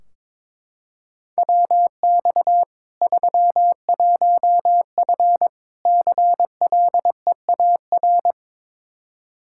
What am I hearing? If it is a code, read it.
Morse code, heard as a keyed tone, about 22 wpm, 700 Hz. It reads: WX 31F CLEAR
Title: Practice QSO - Exchange 2 - Info - 22wpm